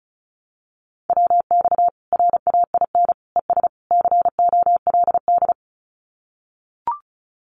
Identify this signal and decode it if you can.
Morse code, heard as a keyed tone, about 34 wpm, 700 Hz. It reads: WX RAIN ES COLD
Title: Common QSO Phrases 35wpm